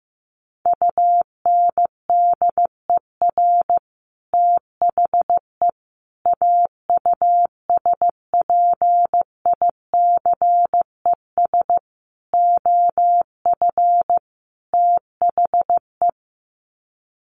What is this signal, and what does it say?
Morse code, heard as a keyed tone, about 15 wpm, 700 Hz. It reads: UNDER THE AUSPICES OF THE
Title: Sets of 5 Words Encore2 15wpm